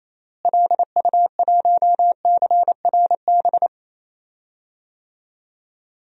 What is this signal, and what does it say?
Morse code, heard as a keyed tone, about 28 wpm, 700 Hz. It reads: LU1CRB
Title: DXCC-100 Call Signs - 6 Characters - Repeated 2x - 28wpm